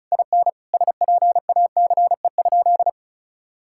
Morse code, heard as a keyed tone, about 36 wpm, 700 Hz. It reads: IN SPACE?
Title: Sentences from Top 1000 Words - Review Part 5 - 35wpm